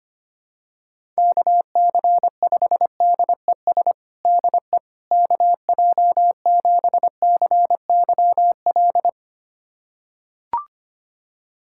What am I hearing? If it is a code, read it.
Morse code, heard as a keyed tone, about 25 wpm, 700 Hz. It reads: KC5DES DE KJ7CYL
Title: USA Callsign DE Callsign - 6 Characters - 25wpm